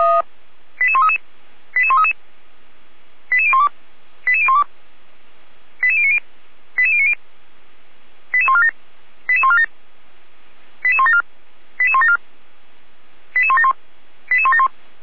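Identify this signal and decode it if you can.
Five-tone selective call (ZVEI), heard as quick runs of five tones, tones of about 70 ms. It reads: 88133 88133 88012 88012 88090 88090 88147 88147 88174 88174 88171 88171
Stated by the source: ZVEI five-tone call received on SDR, 455 channel, logged by feuerwehr-orlamuende.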